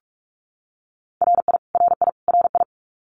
Morse code, heard as a keyed tone, about 44 wpm, 700 Hz. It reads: RI RI RI
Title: US States Repeated 3x 45wpm